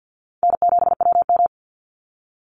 Morse code, heard as a keyed tone, about 50 wpm, 700 Hz. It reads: N7WM
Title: NA SOTA Chaser Callsigns 50wpm